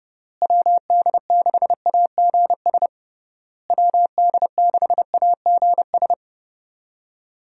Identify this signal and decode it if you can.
Morse code, heard as a keyed tone, about 30 wpm, 700 Hz. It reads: WD6AGS WD6AGS
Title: USA Call Signs - 6 Characters - Repeated 3x - 30wpm